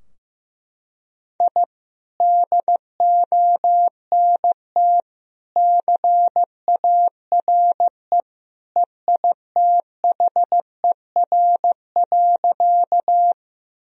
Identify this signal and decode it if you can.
Morse code, heard as a keyed tone, about 16 wpm, 700 Hz. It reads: I DONT CARE EITHER.
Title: Sentences from Top 1000 Words - Review Part 4 - 15wpm